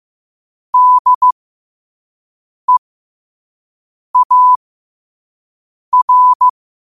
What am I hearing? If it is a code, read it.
Morse code, heard as a keyed tone, about 15 wpm, spaced as for 5 wpm, 1000 Hz. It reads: DEAR
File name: qotdfiles / 5wpm